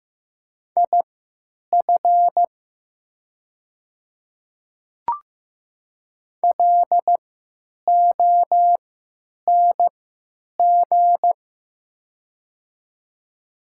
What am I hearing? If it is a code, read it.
Morse code, heard as a keyed tone, about 15 wpm, spaced as for 9 wpm, 700 Hz. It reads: IF LONG
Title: 3x Character Spacing - Top 100 Words - 15wpm